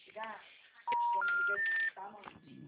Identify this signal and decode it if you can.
Telephone call progress tone, heard as a telephone network's tone, with speech. Special information tone.